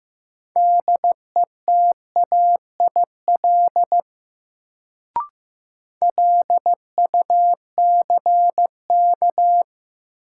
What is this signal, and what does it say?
Morse code, heard as a keyed tone, about 15 wpm, 700 Hz. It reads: DETAIL LUCK
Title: Sentences from Top 1500 Words - New Word Drill - 15wpm